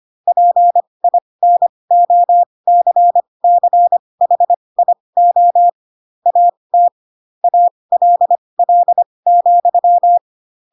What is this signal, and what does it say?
Morse code, heard as a keyed tone, about 25 wpm, spaced as for 20 wpm, 700 Hz. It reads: PINOCCHIO AT ALL,
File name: Pinocchio / Pinocchio ch5 15 25 ik1hge